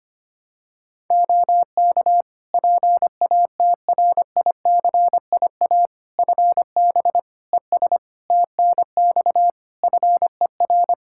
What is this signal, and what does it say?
Morse code, heard as a keyed tone, about 25 wpm, 700 Hz. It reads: OK PATRICIA FB ES TNX FER
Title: Practice QSO - Exchange 2 - Info - 25wpm